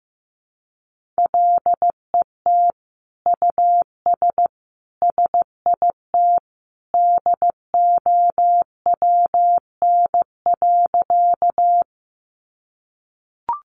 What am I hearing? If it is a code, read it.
Morse code, heard as a keyed tone, about 15 wpm, 700 Hz. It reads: LET US SIT DOWN.
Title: Sentences from Top 500 Words 15wpm